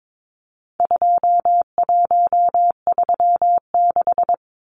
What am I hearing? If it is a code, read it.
Morse code, heard as a keyed tone, about 22 wpm, 700 Hz. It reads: 2136
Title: Numbers - 4-Digit 22wpm